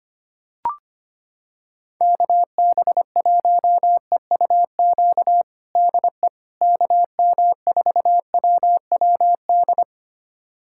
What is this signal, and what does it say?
Morse code, heard as a keyed tone, about 25 wpm, 700 Hz. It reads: KB1EUQ DE KM4WWD
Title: USA Callsign DE Callsign - 6 Characters - 25wpm